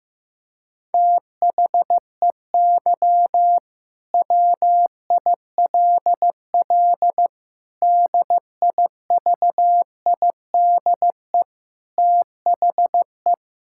Morse code, heard as a keyed tone, about 15 wpm, 700 Hz. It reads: THEY WILL DIVIDE THE
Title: Sentences from Top 1600 Words 15wpm